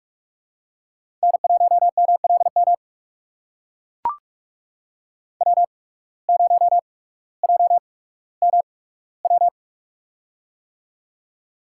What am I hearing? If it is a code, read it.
Morse code, heard as a keyed tone, about 44 wpm, spaced as for 19 wpm, 690 Hz. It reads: N1MLM W0JMW
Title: USA Call Signs - 5 Characters - 8x Character Spacing - 45wpm